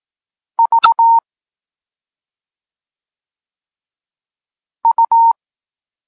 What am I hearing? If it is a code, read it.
Morse code, heard as a keyed tone, about 18 wpm, 920 Hz. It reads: V U